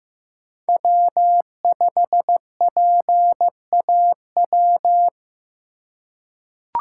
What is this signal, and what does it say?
Morse code, heard as a keyed tone, about 15 wpm, 700 Hz. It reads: W5PAW
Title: USA Call Signs - 2x Character Spacing - 15wpm